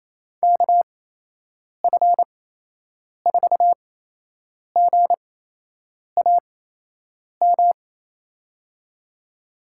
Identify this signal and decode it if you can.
Morse code, heard as a keyed tone, about 28 wpm, spaced as for 8 wpm, 700 Hz. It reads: KF4GAM
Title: USA Call Signs - 6 Characters - 8x Character Spacing - 28wpm